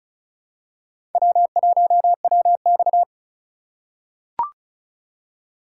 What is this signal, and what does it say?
Morse code, heard as a keyed tone, about 34 wpm, 700 Hz. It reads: W1WX
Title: USA Call Signs - 4 Characters - 2x Character Spacing - 35wpm